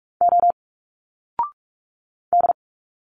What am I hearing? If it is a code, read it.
Morse code, heard as a keyed tone, about 46 wpm, 700 Hz. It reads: O D
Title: Single Letters 45wpm